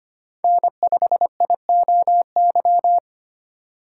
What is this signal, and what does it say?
Morse code, heard as a keyed tone, about 25 wpm, 700 Hz. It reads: N5IOY
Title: USA Call Signs - 5 Characters - 4x Character Spacing - 25wpm